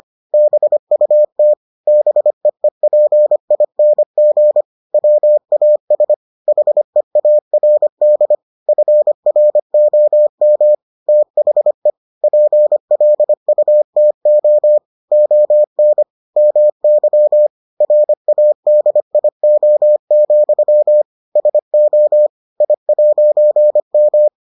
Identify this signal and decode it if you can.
Morse code, heard as a keyed tone, about 25 wpm, 600 Hz. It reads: BUT BEEPING WAS HEARD FROM THE PLUTO ON MY RADIO, SO I'M